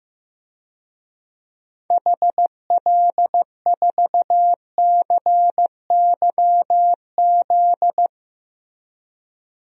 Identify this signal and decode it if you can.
Morse code, heard as a keyed tone, about 15 wpm, 700 Hz. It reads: HL4CYZ
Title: DXCC-100 Call Signs - 6 Characters - 15wpm